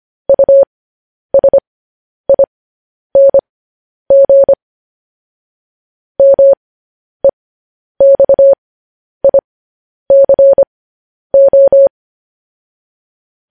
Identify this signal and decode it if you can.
Morse code, heard as a keyed tone, about 25 wpm, spaced as for 10 wpm, 550 Hz. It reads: USING MEXICO
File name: podcast / News headlines-1x-25x10